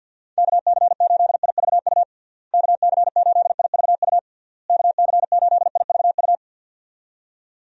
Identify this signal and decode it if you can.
Morse code, heard as a keyed tone, about 51 wpm, 700 Hz. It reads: KC8IVU KC8IVU KC8IVU
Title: USA Call Signs - 6 Characters - Repeated 3x - 50wpm